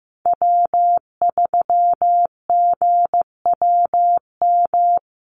Morse code, heard as a keyed tone, about 15 wpm, 700 Hz. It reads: W3GWM
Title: Call Signs Repeated 2x - 15wpm